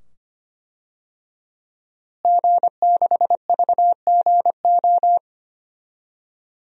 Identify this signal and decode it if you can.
Morse code, heard as a keyed tone, about 25 wpm, 700 Hz. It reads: G6VGO
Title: DXCC-100 Call Signs - 5 Characters - Repeated 2x - 25wpm